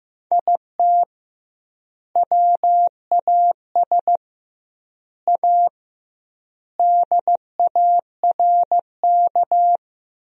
Morse code, heard as a keyed tone, about 15 wpm, 700 Hz. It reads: IT WAS A DARK
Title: 2x Word-Spacing - Sentences from Top 500 Words - 15wpm